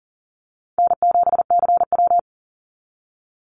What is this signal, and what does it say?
Morse code, heard as a keyed tone, about 38 wpm, 700 Hz. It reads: N7CW
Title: NA SOTA Chaser Callsigns 40wpm